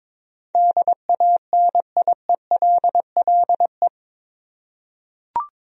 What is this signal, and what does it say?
Morse code, heard as a keyed tone, about 23 wpm, 700 Hz. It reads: DANIELLE
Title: USA Common Names - 22wpm